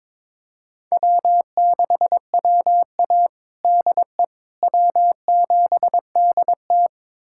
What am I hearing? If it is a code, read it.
Morse code, heard as a keyed tone, about 22 wpm, 700 Hz. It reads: W6WA DE W7DT
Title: USA Callsign DE Callsign - 4 Characters - 22wpm